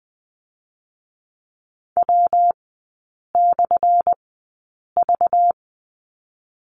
Morse code, heard as a keyed tone, about 20 wpm, spaced as for 8 wpm, 690 Hz. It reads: W/V
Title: 198 Lesson 32e - Letter k - Three Characters 20wpm